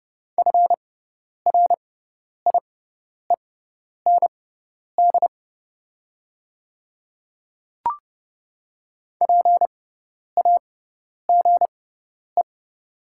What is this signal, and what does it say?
Morse code, heard as a keyed tone, about 30 wpm, spaced as for 10 wpm, 700 Hz. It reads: FRIEND PAGE
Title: 6x Character Spacing - Top 500 Words - 30wpm